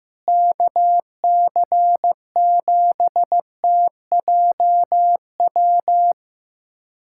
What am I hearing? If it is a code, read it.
Morse code, heard as a keyed tone, about 16 wpm, 700 Hz. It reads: KC7TJW
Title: USA Callsign DE Callsign - 6 Characters - 15wpm